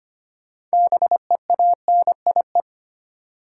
Morse code, heard as a keyed tone, about 25 wpm, 690 Hz. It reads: BEANIE
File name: practice / 25wpm / Easy Words with Hints - 25wpm